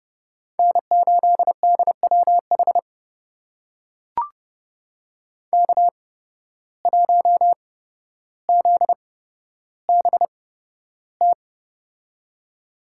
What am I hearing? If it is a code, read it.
Morse code, heard as a keyed tone, about 30 wpm, spaced as for 13 wpm, 700 Hz. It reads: N8DWH K1ZBT
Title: USA Call Signs - 5 Characters - 8x Character Spacing - 30wpm